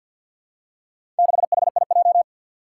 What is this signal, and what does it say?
Morse code, heard as a keyed tone, about 49 wpm, 700 Hz. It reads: 6HIJ
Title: Four Challenging Letter-Number Combinations 50wpm